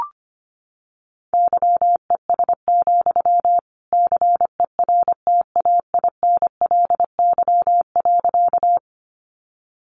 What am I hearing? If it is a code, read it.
Morse code, heard as a keyed tone, about 25 wpm, 700 Hz. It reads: YES, CERTAINLY.
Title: Sentences from Top 900 Words 25wpm